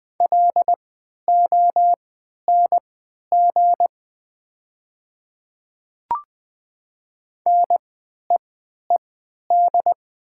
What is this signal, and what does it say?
Morse code, heard as a keyed tone, about 20 wpm, spaced as for 11 wpm, 700 Hz. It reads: LONG NEED
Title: 3x Character Spacing - Top 300 Words - 20wpm